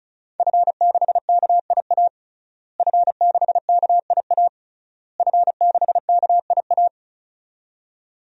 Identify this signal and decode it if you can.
Morse code, heard as a keyed tone, about 34 wpm, 700 Hz. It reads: F6KIA F6KIA F6KIA
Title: DXCC-100 Call Signs - 5 Characters - Repeated 3x - 35wpm